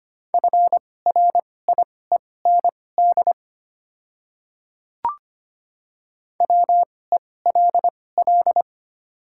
Morse code, heard as a keyed tone, about 25 wpm, spaced as for 18 wpm, 690 Hz. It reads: FRIEND WELL
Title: 2x Character Spacing - Top 300 Words - 25wpm